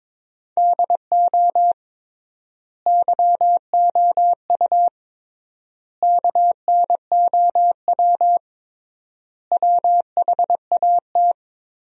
Morse code, heard as a keyed tone, about 22 wpm, 700 Hz. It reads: DO YOU KNOW WHAT
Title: 3x Word-Spacing - Sentences from Top 100 Words - 22wpm